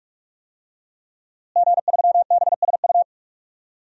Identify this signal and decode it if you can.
Morse code, heard as a keyed tone, about 44 wpm, 700 Hz. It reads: G3BSU
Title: DXCC-100 Call Signs - 5 Characters - Repeated 3x - 45wpm